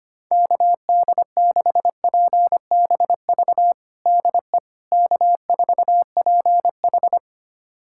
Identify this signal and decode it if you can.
Morse code, heard as a keyed tone, about 25 wpm, 700 Hz. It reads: KD6PBV DE K4PH
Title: USA Callsign DE Callsign 25wpm